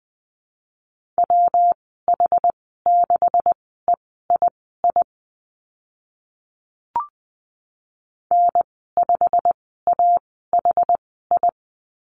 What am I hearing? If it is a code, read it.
Morse code, heard as a keyed tone, about 20 wpm, spaced as for 15 wpm, 700 Hz. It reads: WH6EII N5AHI
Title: 161 Lesson 26g - Letter g - Call Signs 20wpm